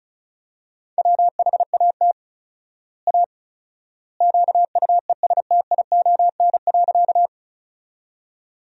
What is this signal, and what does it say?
Morse code, heard as a keyed tone, about 36 wpm, 700 Hz. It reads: WHAT A QUESTION.
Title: 4x Word-Spacing - Sentences from Top 400 Words - 35wpm